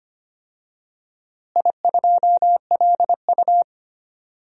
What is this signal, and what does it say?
Morse code, heard as a keyed tone, about 25 wpm, 700 Hz. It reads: I2LU
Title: DXCC-100 Call Signs - 4 Characters - Repeated 2x - 25wpm